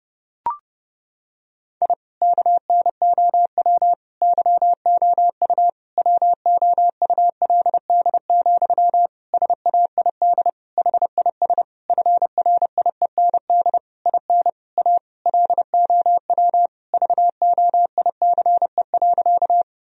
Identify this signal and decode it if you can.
Morse code, heard as a keyed tone, about 30 wpm, 700 Hz. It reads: I KNOW YOU WOULD, SAID HIS FRIEND IN A LOW VOICE.
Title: Sentences from Top 1000 Words - Review Part 1 - 30wpm